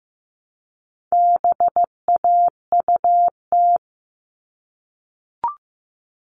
Morse code, heard as a keyed tone, about 15 wpm, 700 Hz. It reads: BAUT
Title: All Cut Numbers - 4-Digit 15wpm